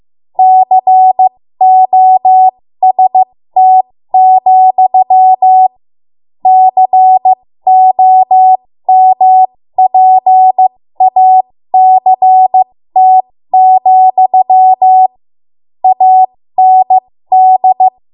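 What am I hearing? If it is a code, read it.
Morse code, heard as a keyed tone, about 15 wpm, spaced as for 13 wpm, 750 Hz. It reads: COST, COMPACT, AND
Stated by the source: 13 wpm Morse code practice archive transmission.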